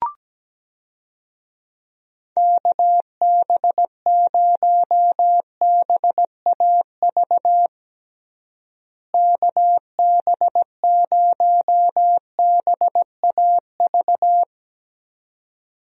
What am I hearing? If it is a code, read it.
Morse code, heard as a keyed tone, about 17 wpm, 700 Hz. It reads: KB0BAV KB0BAV
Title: USA Call Signs - 6 Characters - Repeated 2x - 17wpm